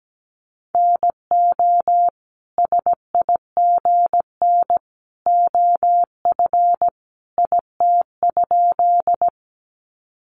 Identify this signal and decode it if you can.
Morse code, heard as a keyed tone, about 17 wpm, 700 Hz. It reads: NO SIGN OF IT?
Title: Sentences from Top 1000 Words 17wpm